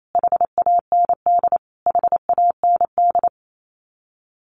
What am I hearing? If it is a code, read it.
Morse code, heard as a keyed tone, about 28 wpm, 700 Hz. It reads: HAND HAND
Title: Top 300 Words Repeated 28wpm